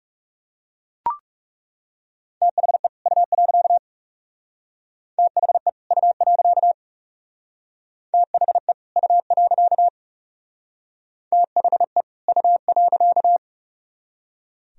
Morse code, heard as a keyed tone, about 36 wpm, 700 Hz. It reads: THE U. THE U. THE U. THE U.
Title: Speed-Racing - Sets of 2 Words 30wpm